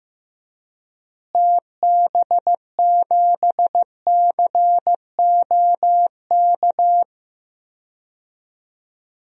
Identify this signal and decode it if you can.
Morse code, heard as a keyed tone, about 16 wpm, 700 Hz. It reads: TB7COK
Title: DXCC-100 Call Signs - 6 Characters - Repeated 3x - 15wpm